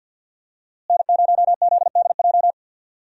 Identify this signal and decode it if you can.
Morse code, heard as a keyed tone, about 47 wpm, 700 Hz. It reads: N0ZDJ
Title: USA Call Signs - 5 Characters - Repeated 2x - 50wpm